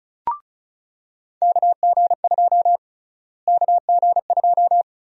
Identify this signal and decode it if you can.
Morse code, heard as a keyed tone, about 35 wpm, 700 Hz. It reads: KG2 KG2
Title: USA Call Sign Prefixes - Repeated 2x - 35wpm